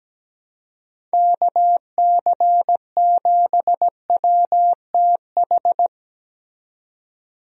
Morse code, heard as a keyed tone, about 17 wpm, 700 Hz. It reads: KC7WTH
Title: USA Call Signs - 6 Characters - 17wpm